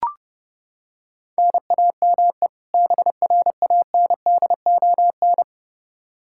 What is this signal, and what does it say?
Morse code, heard as a keyed tone, about 30 wpm, 700 Hz. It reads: NAME BRANDON
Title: K1USN SST QSO Practice 30wpm